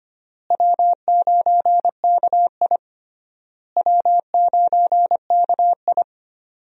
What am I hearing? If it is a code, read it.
Morse code, heard as a keyed tone, about 25 wpm, 700 Hz. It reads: W9KI W9KI
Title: USA Call Signs - 4 Characters - Repeated 3x - 25wpm